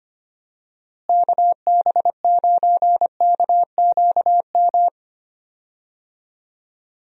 Morse code, heard as a keyed tone, about 25 wpm, 700 Hz. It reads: KB9KQM
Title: USA Call Signs - 6 Characters - Repeated 2x - 25wpm